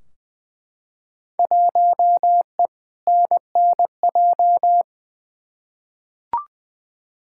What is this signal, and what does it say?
Morse code, heard as a keyed tone, about 20 wpm, 700 Hz. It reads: 1E NNJ
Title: Field Day Exchanges - 20wpm